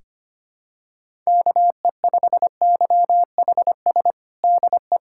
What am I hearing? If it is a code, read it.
Morse code, heard as a keyed tone, about 25 wpm, 700 Hz. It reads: KE5YHS DE
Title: USA Callsign DE Callsign - 6 Characters - 25wpm